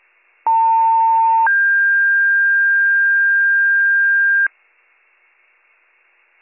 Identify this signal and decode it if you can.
Two-tone page, heard as a one-second tone, then a three-second tone, about 900 then 1600 Hz. It is heard from 0.5 to 4.5 s.